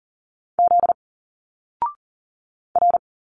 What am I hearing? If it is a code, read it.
Morse code, heard as a keyed tone, about 38 wpm, 700 Hz. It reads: Z R
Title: Single Letters 40wpm